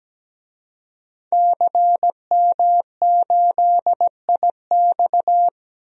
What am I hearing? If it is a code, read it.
Morse code, heard as a keyed tone, about 17 wpm, 700 Hz. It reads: CM8IX
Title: DXCC-100 Call Signs - 5 Characters - Repeated 2x - 17wpm